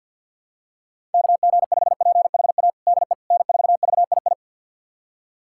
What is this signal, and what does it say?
Morse code, heard as a keyed tone, about 52 wpm, 700 Hz. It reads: KG5PHA DE N4VII
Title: USA Callsign DE Callsign 50wpm